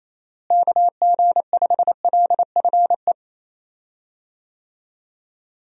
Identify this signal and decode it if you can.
Morse code, heard as a keyed tone, about 27 wpm, 700 Hz. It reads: KG5LFE
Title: USA Call Signs - 6 Characters - Repeated 3x - 28wpm